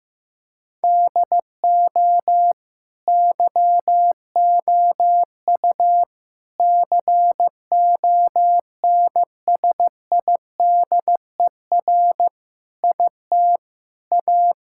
Morse code, heard as a keyed tone, about 16 wpm, 700 Hz. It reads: DO YOU CONSIDER IT A
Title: Sentences from Top 1000 Words - Review Part 6 - 15wpm